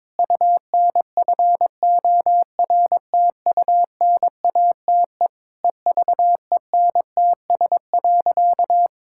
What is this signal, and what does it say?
Morse code, heard as a keyed tone, about 22 wpm, 700 Hz. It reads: UNFORTUNATE EVENTS.
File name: practice / 22wpm / Sentences with 8-Word Phrases - 22wpm